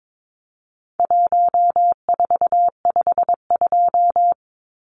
Morse code, heard as a keyed tone, about 22 wpm, 700 Hz. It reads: 1452
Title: Numbers - Contest Style - 4-Digit 22wpm